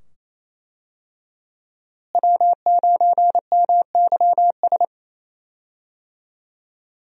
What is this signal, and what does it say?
Morse code, heard as a keyed tone, about 28 wpm, 700 Hz. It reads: W9MYS
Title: USA Call Signs - 5 Characters - Repeated 2x - 28wpm